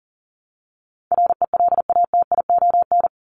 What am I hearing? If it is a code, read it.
Morse code, heard as a keyed tone, about 40 wpm, 700 Hz. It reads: RELATION
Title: Top 1751-2000 Words 40wpm